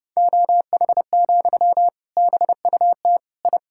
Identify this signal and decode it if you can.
Morse code, heard as a keyed tone, about 28 wpm, 700 Hz. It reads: OH, BUT I
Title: Sentences from Top 1000 Words - Review Part 6 - 30wpm